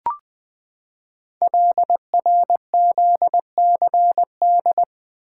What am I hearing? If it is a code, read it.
Morse code, heard as a keyed tone, about 20 wpm, 700 Hz. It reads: LRZCD
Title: Code Groups - 5 Letters - 20wpm